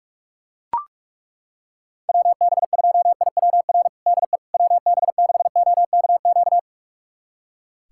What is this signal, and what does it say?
Morse code, heard as a keyed tone, about 41 wpm, 700 Hz. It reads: WB2IWR DE WB6OKQ
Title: USA Callsign DE Callsign 45wpm